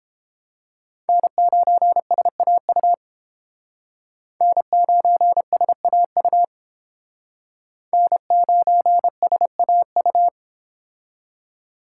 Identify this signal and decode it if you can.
Morse code, heard as a keyed tone, about 30 wpm, 700 Hz. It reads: N9SAU N9SAU N9SAU